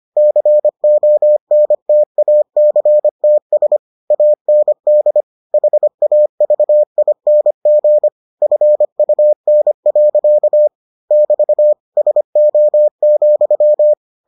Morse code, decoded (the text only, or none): CONTACTS AND HAVING FUN. <BT> SO,